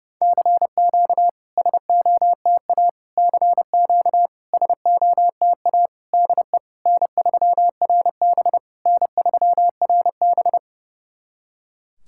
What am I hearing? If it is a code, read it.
Morse code, heard as a keyed tone, about 30 wpm, 690 Hz. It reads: CQ SOTA CQ SOTA DE N3RB N3RB
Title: SOTA QSO Practice 30wpm